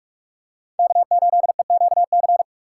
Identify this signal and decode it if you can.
Morse code, heard as a keyed tone, about 46 wpm, 700 Hz. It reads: K8EQC
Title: USA Call Signs - 5 Characters - 6x Character Spacing - 45wpm